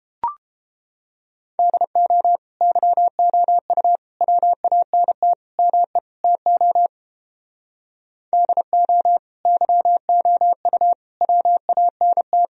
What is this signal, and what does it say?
Morse code, heard as a keyed tone, about 29 wpm, 700 Hz. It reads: DO YOU WANT ME TO DO YOU WANT